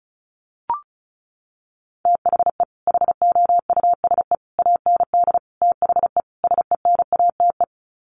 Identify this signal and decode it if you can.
Morse code, heard as a keyed tone, about 34 wpm, 700 Hz. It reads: THE HOUSE AND THE SENATE
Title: Sets of 5 Words Encore2 35wpm